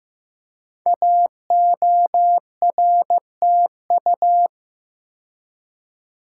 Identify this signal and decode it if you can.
Morse code, heard as a keyed tone, about 15 wpm, 700 Hz. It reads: AORTU
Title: Code Groups - 5 Letters - 15wpm